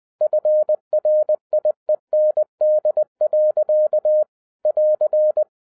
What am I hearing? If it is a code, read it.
Morse code, heard as a keyed tone, about 20 wpm, 610 Hz. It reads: FRIEND. <AR>
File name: 2011-07-mp3 / n58 BACK FROM A BIG ACCIDENT 20wpm